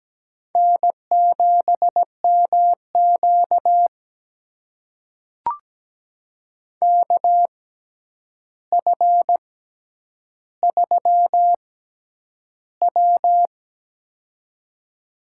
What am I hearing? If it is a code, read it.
Morse code, heard as a keyed tone, about 17 wpm, spaced as for 9 wpm, 700 Hz. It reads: N7MQ KF3W
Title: USA Call Signs - 4 Characters - 6x Character Spacing - 17wpm